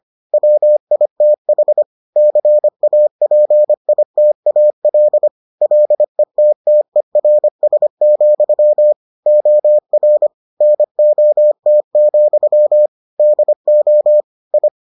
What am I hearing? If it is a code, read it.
Morse code, heard as a keyed tone, about 25 wpm, 600 Hz. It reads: WITH CAPITAL LETTERS, OR NOT, DO I